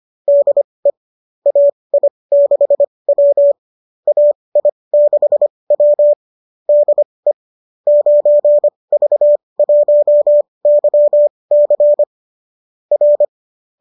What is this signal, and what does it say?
Morse code, heard as a keyed tone, about 25 wpm, spaced as for 20 wpm, 570 Hz. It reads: DE AI6W AI6W DE 9V1YC R
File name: qso207 20